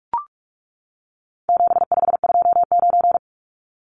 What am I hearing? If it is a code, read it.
Morse code, heard as a keyed tone, about 44 wpm, 700 Hz. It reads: 7529
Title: Numbers - 4-Digit 45wpm